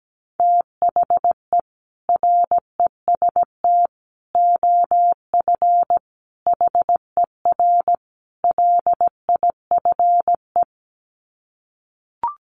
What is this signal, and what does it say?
Morse code, heard as a keyed tone, about 17 wpm, 700 Hz. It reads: THE REST OF HER LIFE
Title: Sets of 5 Words Encore 17wpm